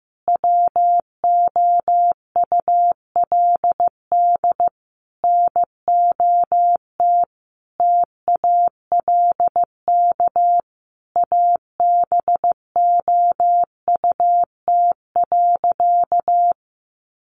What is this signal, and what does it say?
Morse code, heard as a keyed tone, about 15 wpm, 700 Hz. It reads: WOULD NOT TALK ABOUT.
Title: Sentences from Top 800 Words 15wpm